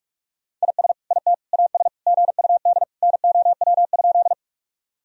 Morse code, heard as a keyed tone, about 45 wpm, 700 Hz. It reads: IS IT AS GUD NOW?
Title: Sentences from Top 200 Words and Abbreviations 45wpm